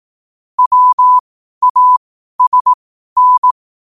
Morse code, heard as a keyed tone, about 18 wpm, spaced as for 12 wpm, 1000 Hz. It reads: WASN
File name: qotdfiles / 12wpm